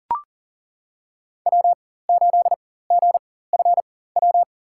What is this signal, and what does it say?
Morse code, heard as a keyed tone, about 39 wpm, spaced as for 19 wpm, 700 Hz. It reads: W8GFW
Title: USA Call Signs - 5 Characters - 4x Character Spacing - 40wpm